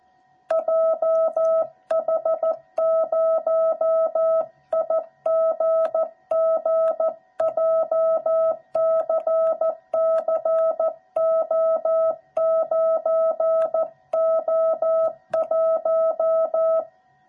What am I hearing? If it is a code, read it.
Morse code, heard as a keyed tone, about 14 wpm, 660 Hz. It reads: JH0IGG JCCO9O1
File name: jh8cbh / 2021cw / cw06b